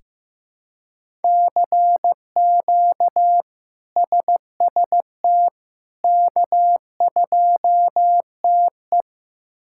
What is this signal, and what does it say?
Morse code, heard as a keyed tone, about 15 wpm, 700 Hz. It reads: CQ SST K2TE
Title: K1USN SST QSO Practice - 15wpm